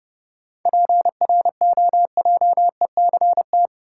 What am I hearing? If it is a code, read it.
Morse code, heard as a keyed tone, about 32 wpm, 700 Hz. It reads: PROJECT
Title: Sentences from Top 1400 Words - New Word Drill - 30wpm